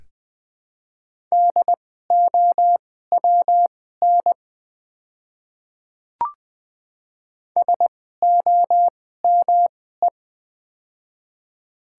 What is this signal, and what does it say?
Morse code, heard as a keyed tone, about 20 wpm, spaced as for 15 wpm, 700 Hz. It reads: DOWN SOME
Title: 2x Character Spacing - Top 100 Words - 20wpm